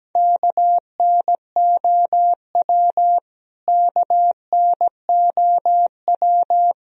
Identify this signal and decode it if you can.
Morse code, heard as a keyed tone, about 17 wpm, 700 Hz. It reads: KNOW KNOW
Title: Top 100 Words Repeated 17wpm